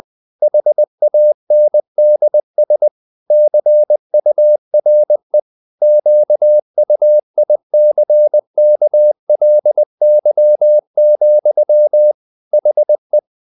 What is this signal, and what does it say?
Morse code, heard as a keyed tone, about 20 wpm, 600 Hz. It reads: HANDS CURE QUICKLY, HE
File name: the old man and the sea part 2 20wpm